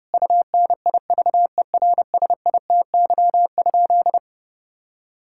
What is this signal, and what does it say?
Morse code, heard as a keyed tone, about 30 wpm, 700 Hz. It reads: UNIVERSITY?
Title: Sentences from Top 1000 Words - Review Part 5 - 30wpm